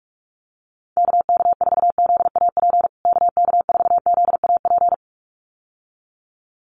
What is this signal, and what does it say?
Morse code, heard as a keyed tone, about 45 wpm, 700 Hz. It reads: KK4ZAP KK4ZAP
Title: USA Call Signs Repeated 2x 45wpm